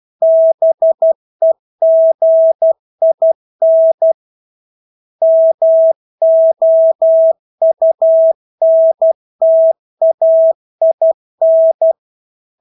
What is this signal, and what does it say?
Morse code, heard as a keyed tone, about 12 wpm, 650 Hz. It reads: BEGIN MOUNTAIN
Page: Engelske ord 12 wpm